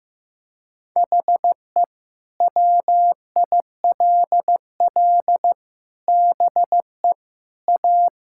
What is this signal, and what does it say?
Morse code, heard as a keyed tone, about 16 wpm, 700 Hz. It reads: HE WILL BE A
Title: Sentences from Top 1000 Words - Review Part 5 - 15wpm